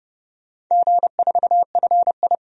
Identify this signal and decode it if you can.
Morse code, heard as a keyed tone, about 32 wpm, 700 Hz. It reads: G4FI
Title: DXCC-100 Call Signs - 4 Characters - 30wpm